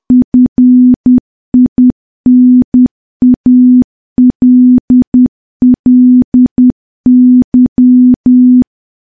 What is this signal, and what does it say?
Morse code, heard as a keyed tone, about 10 wpm, 250 Hz. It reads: FINALLY
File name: morse / morse-training-10WPM